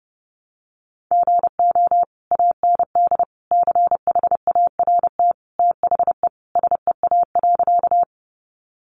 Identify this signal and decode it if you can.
Morse code, heard as a keyed tone, about 30 wpm, 700 Hz. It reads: GO AND CHART THE SEA.
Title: Sentences from Top 1300 Words 30wpm